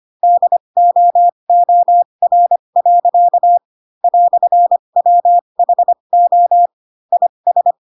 Morse code, heard as a keyed tone, about 25 wpm, spaced as for 22 wpm, 700 Hz. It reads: DOOR. "WHO IS
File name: Pinocchio / Pinocchio ch2 18 25 ik1hge